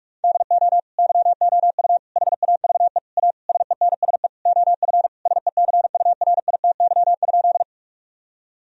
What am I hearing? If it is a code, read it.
Morse code, heard as a keyed tone, about 46 wpm, 700 Hz. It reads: DO YOU HAVE A SENSE OF SECURITY?
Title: Sentences from Top 1000 Words - Review Part 4 - 45wpm